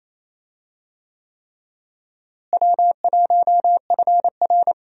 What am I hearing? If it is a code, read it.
Morse code, heard as a keyed tone, about 28 wpm, 700 Hz. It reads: W1FR
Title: USA Call Signs - 4 Characters - Repeated 3x - 28wpm